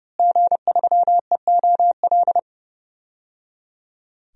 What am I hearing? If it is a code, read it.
Morse code, heard as a keyed tone, about 30 wpm, 700 Hz. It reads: G3EOL
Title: DXCC-100 Call Signs - 5 Characters - Repeated 3x - 30wpm